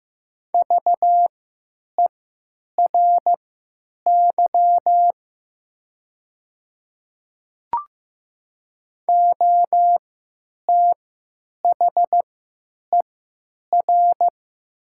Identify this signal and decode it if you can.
Morse code, heard as a keyed tone, about 15 wpm, spaced as for 9 wpm, 700 Hz. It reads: VERY OTHER
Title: 3x Character Spacing - Top 200 Words - 15wpm